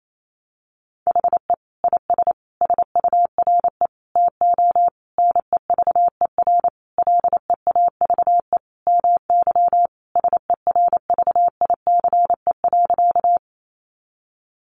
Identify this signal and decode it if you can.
Morse code, heard as a keyed tone, about 28 wpm, 700 Hz. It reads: HE IS SURE TO NEVER LEAVE MY SERVICE.
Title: Sentences from Top 700 Words 28wpm